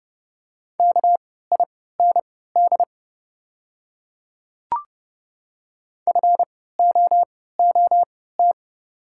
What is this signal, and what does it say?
Morse code, heard as a keyed tone, about 29 wpm, spaced as for 17 wpm, 690 Hz. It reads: KIND FOOT
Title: 3x Character Spacing - Top 500 Words - 30wpm